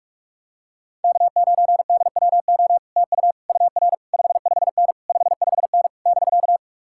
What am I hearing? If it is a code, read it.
Morse code, heard as a keyed tone, about 37 wpm, 700 Hz. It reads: K9DWO TU UR 55N 55N <BK>
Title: Speed-Racing Common QSO Phrases - 30wpm